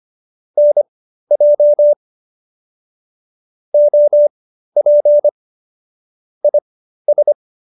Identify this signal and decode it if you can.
Morse code, heard as a keyed tone, about 25 wpm, spaced as for 13 wpm, 590 Hz. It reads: NJ OP IS